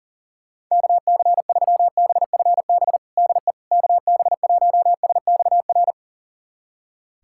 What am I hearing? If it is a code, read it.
Morse code, heard as a keyed tone, about 40 wpm, 700 Hz. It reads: KC3BFB DE KB1SXR
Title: USA Callsign DE Callsign - 6 Characters - 40wpm